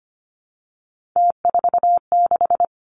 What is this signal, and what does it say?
Morse code, heard as a keyed tone, about 25 wpm, 700 Hz. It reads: T46
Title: Three Challenging Letter-Number Combinations 25wpm